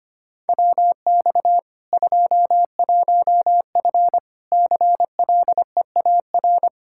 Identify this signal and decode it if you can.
Morse code, heard as a keyed tone, about 25 wpm, 700 Hz. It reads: WX 21F CLEAR
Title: Practice QSO - Exchange 2 - Info - 25wpm